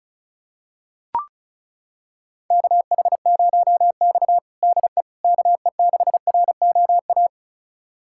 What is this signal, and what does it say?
Morse code, heard as a keyed tone, about 36 wpm, 700 Hz. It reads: KH0X DE KE6ROA
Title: USA Callsign DE Callsign 35wpm